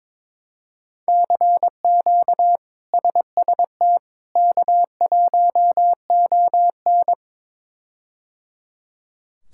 Morse code, heard as a keyed tone, about 22 wpm, 700 Hz. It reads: CQ SST K1ON
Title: K1USN SST QSO Practice - 22wpm